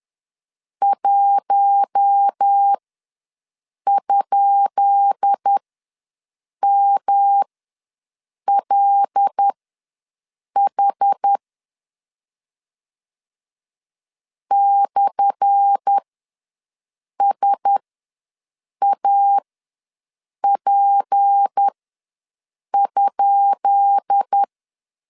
Morse code, heard as a keyed tone, about 10 wpm, spaced as for 6 wpm, 800 Hz. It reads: 1?MLH /SAP?
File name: LEZIONI / 10 SIGNS / 10B INT ws 10wpm 4134